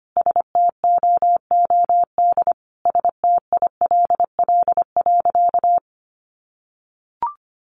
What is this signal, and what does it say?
Morse code, heard as a keyed tone, about 25 wpm, 700 Hz. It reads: STOOD STILL.
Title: Sentences from Top 500 Words 25wpm